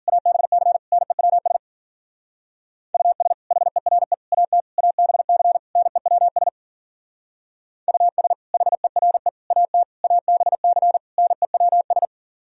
Morse code, heard as a keyed tone, about 49 wpm, 700 Hz. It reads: ABC NEWS US HERE AT ABC NEWS US HERE AT ABC NEWS
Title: Speed-Racing - Sets of 5 Words 40wpm